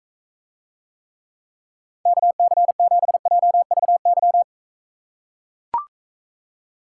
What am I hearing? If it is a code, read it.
Morse code, heard as a keyed tone, about 43 wpm, 700 Hz. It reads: KC7JVY